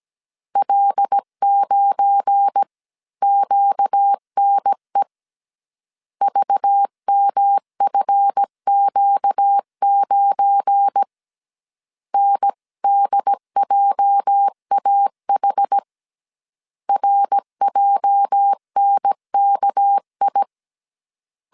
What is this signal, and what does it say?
Morse code, heard as a keyed tone, about 17 wpm, 800 Hz. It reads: L9QNE VMFQ9 NDJAH RJNKI
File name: CW / DET / 1 9 1500C S C 16wpm 2909